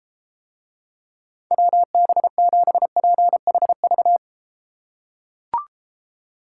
Morse code, heard as a keyed tone, about 32 wpm, 700 Hz. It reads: WB7PHV